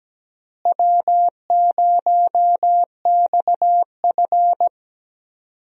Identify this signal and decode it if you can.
Morse code, heard as a keyed tone, about 17 wpm, 700 Hz. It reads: W0XF
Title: USA Call Signs - 4 Characters - Repeated 3x - 17wpm